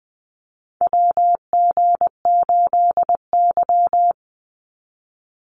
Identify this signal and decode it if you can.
Morse code, heard as a keyed tone, about 20 wpm, 690 Hz. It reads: WG8Y
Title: NA SOTA Chaser Callsigns 20wpm